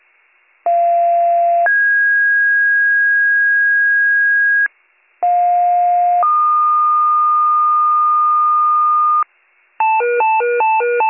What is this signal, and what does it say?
Two-tone page, heard as a one-second tone, then a three-second tone, about 690 then 1650 Hz, about 710 then 1220 Hz, with noise. It is heard from 0.7 to 9.2 s.